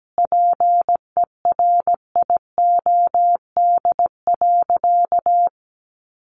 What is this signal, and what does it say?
Morse code, heard as a keyed tone, about 17 wpm, 700 Hz. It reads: PERIOD.
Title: Sentences from Top 800 Words 17wpm